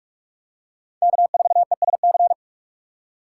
Morse code, heard as a keyed tone, about 46 wpm, 700 Hz. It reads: K4ESC
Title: USA Call Signs - 5 Characters - 6x Character Spacing - 45wpm